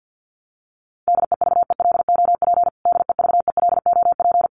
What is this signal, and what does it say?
Morse code, heard as a keyed tone, about 50 wpm, 700 Hz. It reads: DEVELOP DEVELOP
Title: Top 500 Words Repeated 50wpm